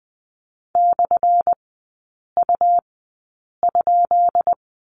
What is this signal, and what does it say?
Morse code, heard as a keyed tone, about 20 wpm, spaced as for 8 wpm, 690 Hz. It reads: /U?
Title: 212 Lesson 34e - Number 8 - Three Characters 20wpm